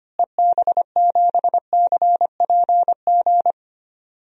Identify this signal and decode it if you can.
Morse code, heard as a keyed tone, about 25 wpm, 700 Hz. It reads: EB7CPG
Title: DXCC-100 Call Signs - 6 Characters - Repeated 2x - 25wpm